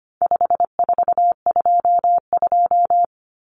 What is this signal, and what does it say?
Morse code, heard as a keyed tone, about 25 wpm, 700 Hz. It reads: 5422
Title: Numbers - 4-Digit 25wpm